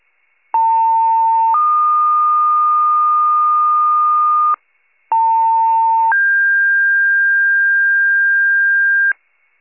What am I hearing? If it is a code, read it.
Two-tone page, heard as a one-second tone, then a three-second tone, about 900 then 1220 Hz, about 900 then 1600 Hz, with noise. It is heard from 0.5 to 9.1 s.
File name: pager / 73 RESCUE / wcalert